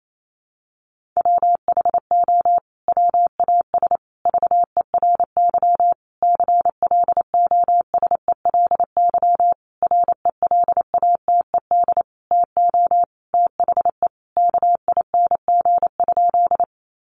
Morse code, heard as a keyed tone, about 28 wpm, 700 Hz. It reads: WHO WAS VERY CLOSELY RELATED TO THE KING?
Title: Sentences from Top 1400 Words 28wpm